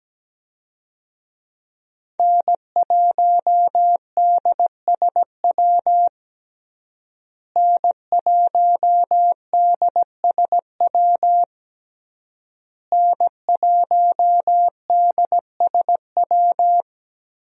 Morse code, heard as a keyed tone, about 17 wpm, 700 Hz. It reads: N1DSW N1DSW N1DSW
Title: USA Call Signs - 5 Characters - Repeated 3x - 17wpm